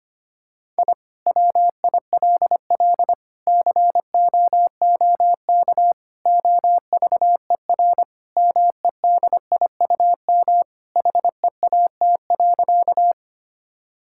Morse code, heard as a keyed tone, about 25 wpm, 700 Hz. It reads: I WILL COOK OVER MEDIUM HEAT.
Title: Sentences from Top 1000 Words - Review Part 6 - 25wpm